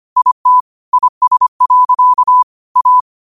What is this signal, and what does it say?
Morse code, heard as a keyed tone, about 25 wpm, 1000 Hz. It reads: IT IS. A
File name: qotdfiles / 25wpm